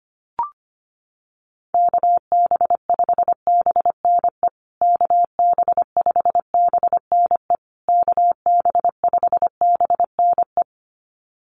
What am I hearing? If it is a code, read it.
Morse code, heard as a keyed tone, about 25 wpm, 700 Hz. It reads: KB5BNE KB5BNE KB5BNE
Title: USA Call Signs Repeated 3x 25wpm